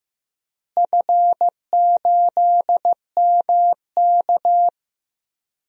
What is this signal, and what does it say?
Morse code, heard as a keyed tone, about 15 wpm, 700 Hz. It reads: F8MK
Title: DXCC-100 Call Signs - 4 Characters - Repeated 2x - 15wpm